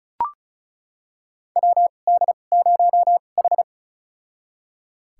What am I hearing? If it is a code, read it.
Morse code, heard as a keyed tone, about 36 wpm, spaced as for 25 wpm, 700 Hz. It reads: WD0H
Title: USA Call Signs - 4 Characters - 2x Character Spacing - 35wpm